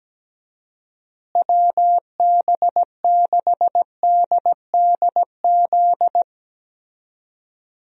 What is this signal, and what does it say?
Morse code, heard as a keyed tone, about 17 wpm, 700 Hz. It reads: WB6DDZ
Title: USA Call Signs - 6 Characters - Repeated 3x - 17wpm